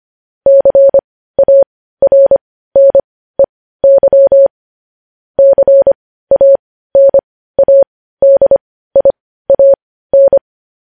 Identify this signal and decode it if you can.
Morse code, heard as a keyed tone, about 25 wpm, spaced as for 15 wpm, 550 Hz. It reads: CARNEY CANADIAN
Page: news headlines in morse code at 15 wpm